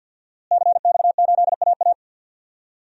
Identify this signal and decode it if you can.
Morse code, heard as a keyed tone, about 53 wpm, 700 Hz. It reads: CX8AA
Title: DXCC-100 Call Signs - 5 Characters - Repeated 3x - 50wpm